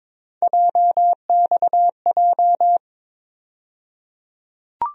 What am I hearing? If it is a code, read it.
Morse code, heard as a keyed tone, about 22 wpm, 700 Hz. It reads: JXJ
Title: USA Call Sign Postfixes - 3 Letters 22wpm